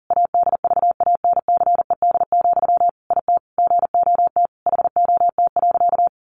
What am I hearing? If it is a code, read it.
Morse code, heard as a keyed tone, about 40 wpm, 700 Hz. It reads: ADVANCED, IT GOT HOT.
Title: Sentences from Top 1200 Words 40wpm